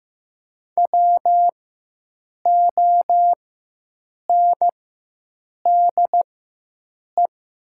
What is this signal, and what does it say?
Morse code, heard as a keyed tone, about 15 wpm, spaced as for 7 wpm, 700 Hz. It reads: WONDE
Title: 4x Character Spacing - Top 500 Words - 15wpm